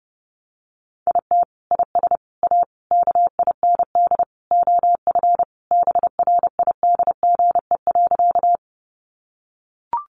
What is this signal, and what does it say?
Morse code, heard as a keyed tone, about 30 wpm, 700 Hz. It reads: IT IS A KIND OF BRIDGE.
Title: Sentences from Top 1200 Words 30wpm